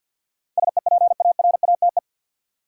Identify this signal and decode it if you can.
Morse code, heard as a keyed tone, about 53 wpm, 700 Hz. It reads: SEPARATE
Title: Sentences from Top 1000 Words - Review Part 7 - New Word Drill - 50wpm